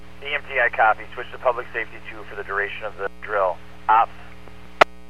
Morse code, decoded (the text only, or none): none